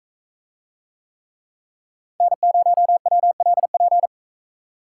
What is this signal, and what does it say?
Morse code, heard as a keyed tone, about 43 wpm, 700 Hz. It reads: N0WLP